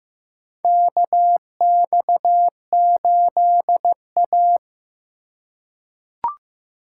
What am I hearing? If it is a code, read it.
Morse code, heard as a keyed tone, about 15 wpm, 700 Hz. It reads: KX8A
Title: USA Call Signs - 4 Characters - 4x Character Spacing - 15wpm